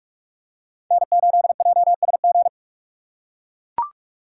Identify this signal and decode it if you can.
Morse code, heard as a keyed tone, about 42 wpm, 700 Hz. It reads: N8JSG
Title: USA Call Signs - 5 Characters - 6x Character Spacing - 45wpm